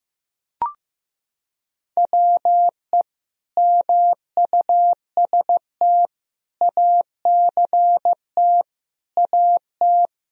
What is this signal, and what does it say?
Morse code, heard as a keyed tone, about 16 wpm, 700 Hz. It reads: WE MUST ACT AT
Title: Sentences from Top 1000 Words - Review Part 1 - 15wpm